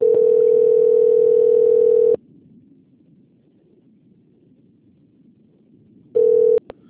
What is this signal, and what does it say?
Ringback tone (North America), 440 + 480 Hz.
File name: out- - - - - .166